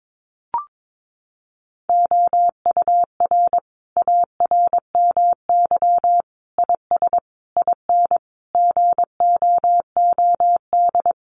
Morse code, heard as a keyed tone, about 22 wpm, 700 Hz. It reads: OUR ARMY IS IN GOOD
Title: Sentences from Top 1500 Words 22wpm